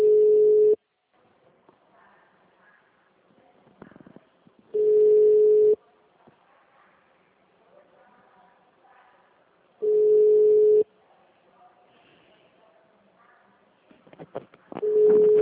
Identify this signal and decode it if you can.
Ringback tone (Europe, 425 Hz).